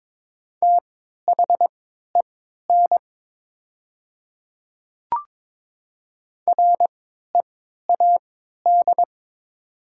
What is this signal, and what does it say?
Morse code, heard as a keyed tone, about 22 wpm, spaced as for 12 wpm, 700 Hz. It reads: THEN READ
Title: 3x Character Spacing - Top 200 Words - 22wpm